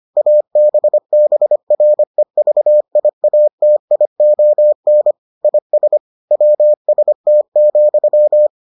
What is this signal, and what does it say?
Morse code, heard as a keyed tone, about 25 wpm, 600 Hz. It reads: ABBREVIATION IS WST,